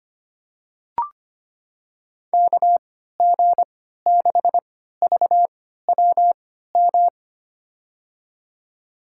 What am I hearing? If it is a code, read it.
Morse code, heard as a keyed tone, about 25 wpm, spaced as for 14 wpm, 690 Hz. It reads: KG6VWM
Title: USA Call Signs - 6 Characters - 3x Character Spacing - 25wpm